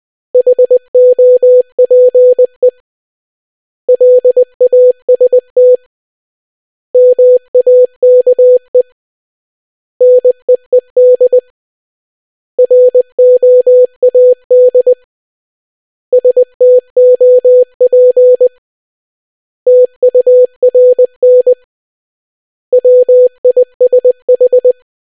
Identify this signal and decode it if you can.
Morse code, heard as a keyed tone, about 20 wpm, 500 Hz. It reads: HOPE LAST MAKE NEED ROAD STOP TURN WISH